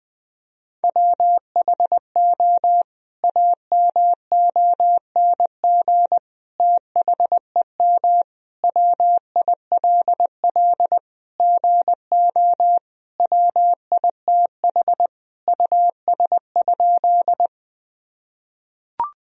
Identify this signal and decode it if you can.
Morse code, heard as a keyed tone, about 20 wpm, 700 Hz. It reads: WHO AMONG THEM WILL GO WITH US?
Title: Sentences from Top 1000 Words - Review Part 3 - 20wpm